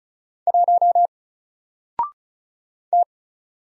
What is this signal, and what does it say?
Morse code, heard as a keyed tone, about 36 wpm, 700 Hz. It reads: 1 T
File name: Single-Letter-Number-35wpm